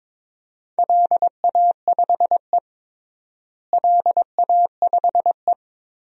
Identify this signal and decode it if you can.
Morse code, heard as a keyed tone, about 21 wpm, 700 Hz. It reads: LA5E LA5E
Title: DXCC-100 Call Signs - 4 Characters - Repeated 3x - 22wpm